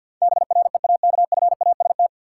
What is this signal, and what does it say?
Morse code, heard as a keyed tone, about 47 wpm, 700 Hz. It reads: BREAKFAST
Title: Sentences from Top 1500 Words - New Word Drill - 50wpm